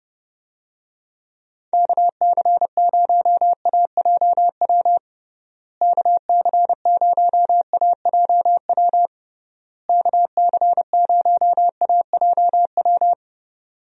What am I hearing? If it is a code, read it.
Morse code, heard as a keyed tone, about 30 wpm, 700 Hz. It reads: KC0AJW KC0AJW KC0AJW
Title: USA Call Signs - 6 Characters - Repeated 3x - 30wpm